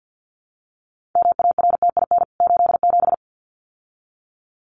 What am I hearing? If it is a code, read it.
Morse code, heard as a keyed tone, about 49 wpm, 700 Hz. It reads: MARTIN 87
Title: CWT Exchange 50wpm